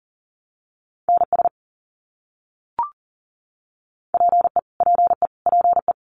Morse code, heard as a keyed tone, about 40 wpm, 700 Hz. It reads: NS PE PE PE
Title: Canadian Provinces and Territories - Repeated 3x - 40wpm